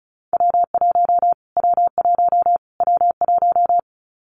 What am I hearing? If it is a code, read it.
Morse code, heard as a keyed tone, about 34 wpm, 700 Hz. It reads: W1 W1 W1
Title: Callsign Prefixes for Top 25 DXCC - Repeated 3x - 35wpm